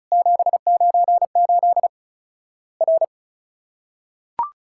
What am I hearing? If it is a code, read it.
Morse code, heard as a keyed tone, about 36 wpm, 700 Hz. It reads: 798 R
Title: Worked All Europe DX Contest - QTC Exchange - 35wpm